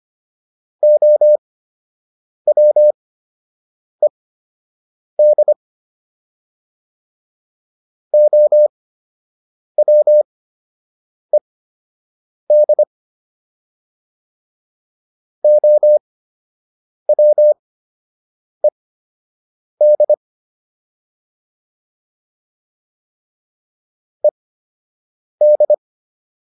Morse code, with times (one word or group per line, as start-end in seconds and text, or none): 0.8-5.5 OWED
8.1-12.8 OWED
15.4-20.1 OWED
24.2-25.7 ED